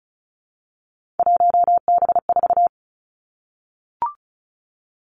Morse code, heard as a keyed tone, about 36 wpm, 700 Hz. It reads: 1B4
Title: Three Challenging Letter-Number Combinations 35wpm